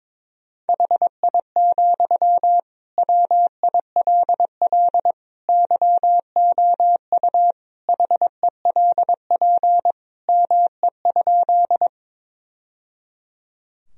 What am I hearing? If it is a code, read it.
Morse code, heard as a keyed tone, about 22 wpm, 700 Hz. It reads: HI, WILL YOU HELP ME?
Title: Sentences from Top 1000 Words - Review Part 3 - 22wpm